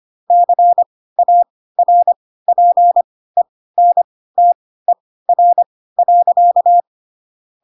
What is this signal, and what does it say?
Morse code, heard as a keyed tone, about 25 wpm, spaced as for 16 wpm, 700 Hz. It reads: CARPENTER.
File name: Pinocchio / Pinocchio ch2 10 25 ik1hge